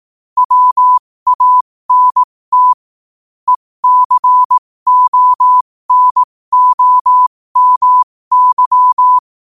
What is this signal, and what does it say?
Morse code, heard as a keyed tone, about 18 wpm, spaced as for 15 wpm, 1000 Hz. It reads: WANT ECONOMY
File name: qotdfiles / 15wpm